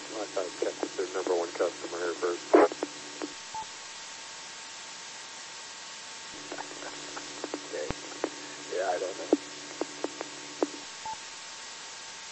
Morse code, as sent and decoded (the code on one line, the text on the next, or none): none